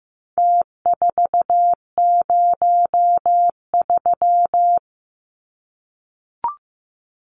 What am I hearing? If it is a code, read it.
Morse code, heard as a keyed tone, about 15 wpm, 700 Hz. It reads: T403
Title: Numbers - Contest Style - 4-Digit 15wpm